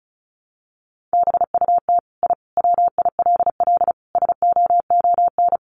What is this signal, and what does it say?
Morse code, heard as a keyed tone, about 34 wpm, 700 Hz. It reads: BUT I WILL SOON
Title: Sentences from Top 1300 Words 35wpm